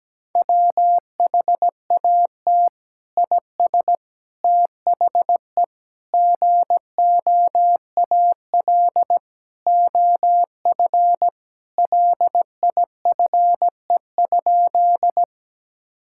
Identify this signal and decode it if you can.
Morse code, heard as a keyed tone, about 17 wpm, 700 Hz. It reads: WHAT IS THE GOAL OF LIFE?
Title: Sentences from Top 1000 Words - Review Part 5 - 17wpm